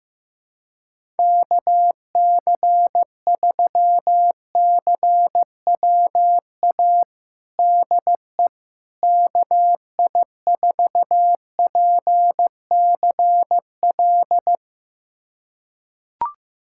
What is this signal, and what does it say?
Morse code, heard as a keyed tone, about 15 wpm, 700 Hz. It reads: KC3CWA DE KI4PCL
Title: USA Callsign DE Callsign 15wpm